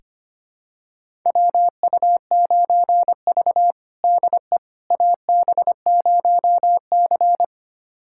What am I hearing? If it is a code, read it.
Morse code, heard as a keyed tone, about 25 wpm, 700 Hz. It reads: WU9V DE AB0C
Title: USA Callsign DE Callsign - 4 Characters - 25wpm